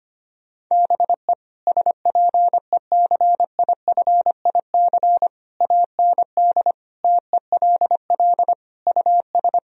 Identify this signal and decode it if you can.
Morse code, heard as a keyed tone, about 25 wpm, 700 Hz. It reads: BE SPECIFIC AND TELL US
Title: Sentences from Top 1000 Words - Review Part 3 - 25wpm